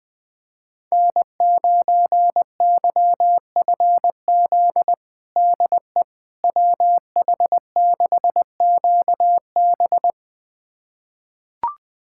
Morse code, heard as a keyed tone, about 21 wpm, 700 Hz. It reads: N9YFZ DE WH6QB
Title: USA Callsign DE Callsign - 5 Characters - 20wpm